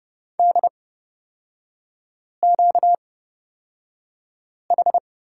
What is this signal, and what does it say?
Morse code, heard as a keyed tone, about 30 wpm, spaced as for 5 wpm, 690 Hz. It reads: DQH
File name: Lesson 38e - Letter q - Three Characters 30wpm